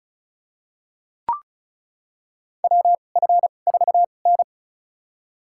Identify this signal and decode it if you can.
Morse code, heard as a keyed tone, about 34 wpm, spaced as for 25 wpm, 700 Hz. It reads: WF4N
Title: USA Call Signs - 4 Characters - 2x Character Spacing - 35wpm